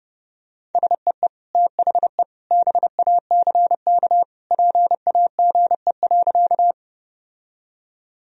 Morse code, heard as a keyed tone, about 32 wpm, 700 Hz. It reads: SEE THE BACK PAGE.
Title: Sentences from Top 1000 Words - Review Part 1 - 30wpm